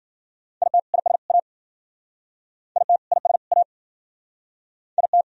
Morse code, heard as a keyed tone, about 54 wpm, 700 Hz. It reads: IT IS A IT IS A IT